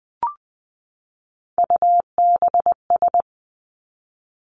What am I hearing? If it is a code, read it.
Morse code, heard as a keyed tone, about 20 wpm, 700 Hz. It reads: UBS
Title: Three Challenging Letter-Number Combinations 20wpm